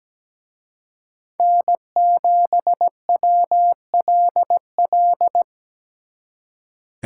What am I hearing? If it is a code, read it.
Morse code, heard as a keyed tone, about 20 wpm, 700 Hz. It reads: N7WLL
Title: Speed Racing - USA 1x3 Call Signs 17wpm